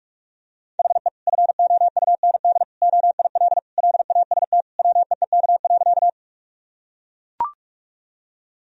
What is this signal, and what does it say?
Morse code, heard as a keyed tone, about 41 wpm, 700 Hz. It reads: HE FOUND OIL LAST WEEK.
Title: Sentences from Top 1000 Words - Review Part 4 - 45wpm